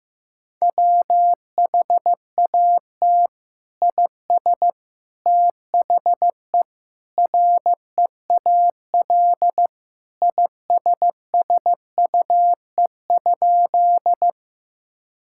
Morse code, heard as a keyed tone, about 15 wpm, 700 Hz. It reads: WHAT IS THE REAL ISSUE?
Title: Sentences from Top 1000 Words - Review Part 6 - 15wpm